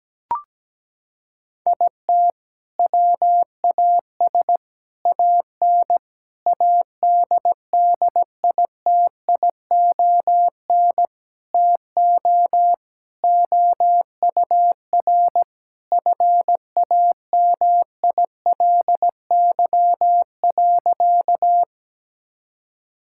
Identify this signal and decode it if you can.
Morse code, heard as a keyed tone, about 17 wpm, 700 Hz. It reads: IT WAS AN ADDITION TO OUR FAMILY.
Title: Sentences from Top 1000 Words - Review Part 6 - 17wpm